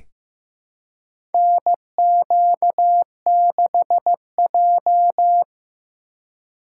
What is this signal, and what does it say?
Morse code, heard as a keyed tone, about 15 wpm, 700 Hz. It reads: NQ6J
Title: USA Call Signs - 4 Characters - 2x Character Spacing - 15wpm